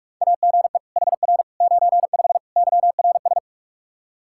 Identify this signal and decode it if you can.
Morse code, heard as a keyed tone, about 44 wpm, 700 Hz. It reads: AGE HR 95 YRS
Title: Practice QSO - Exchange 3 - More Info - 45wpm